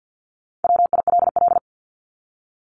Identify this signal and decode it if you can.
Morse code, heard as a keyed tone, about 50 wpm, 700 Hz. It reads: WILL
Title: Top 200 Words 50wpm